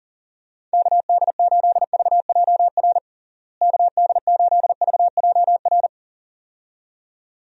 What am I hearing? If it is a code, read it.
Morse code, heard as a keyed tone, about 37 wpm, 700 Hz. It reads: KD8VJR KD8VJR
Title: USA Call Signs - 6 Characters - Repeated 2x - 40wpm